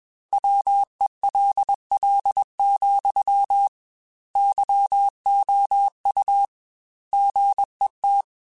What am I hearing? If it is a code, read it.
Morse code, heard as a keyed tone, about 21 wpm, 800 Hz. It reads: WELL, YOU GET
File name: Technical Subjects / Technical Subjects-mp3s / AND Gate @20